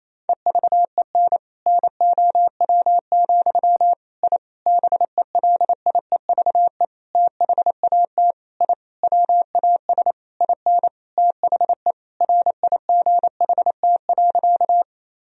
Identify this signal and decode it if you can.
Morse code, heard as a keyed tone, about 28 wpm, 700 Hz. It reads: EVEN NOW, I BELIEVE THAT I WAS IN THE RIGHT.
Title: Sentences from Top 1000 Words - Review Part 1 - 28wpm